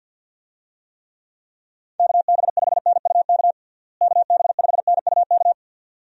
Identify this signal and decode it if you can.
Morse code, heard as a keyed tone, about 46 wpm, 700 Hz. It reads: KB5NUK KB5NUK
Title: USA Call Signs - 6 Characters - Repeated 2x - 50wpm